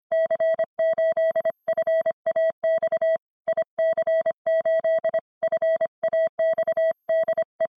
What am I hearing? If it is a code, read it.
Morse code, heard as a keyed tone, about 25 wpm, 650 Hz. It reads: C8FAX IC8FAX DE
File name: ea1rx / sonidos / IC8FAX144CW